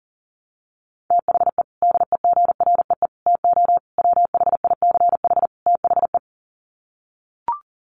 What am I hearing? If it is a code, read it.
Morse code, heard as a keyed tone, about 40 wpm, 700 Hz. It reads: THE DEGREE TO WHICH THE
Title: Sets of 5 Words Encore2 40wpm